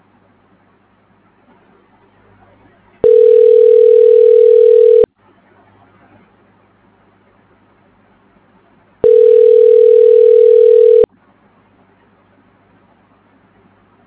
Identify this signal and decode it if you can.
Ringback tone (North America), 440 + 480 Hz.
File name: recordings / REa8f82aa6b3f4b795b6a897ac940c3f89